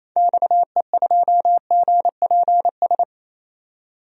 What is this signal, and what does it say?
Morse code, heard as a keyed tone, about 28 wpm, 700 Hz. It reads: XE2GPS
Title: DXCC-100 Call Signs - 6 Characters - 28wpm